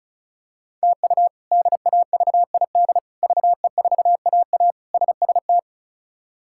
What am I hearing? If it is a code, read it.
Morse code, heard as a keyed tone, about 36 wpm, 700 Hz. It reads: TU DAVID VE4AA SST
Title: K1USN SST QSO Practice - 35wpm